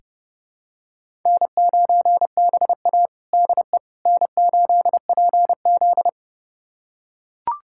Morse code, heard as a keyed tone, about 30 wpm, 700 Hz. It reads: N9BA DE N8PZ
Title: USA Callsign DE Callsign - 4 Characters - 30wpm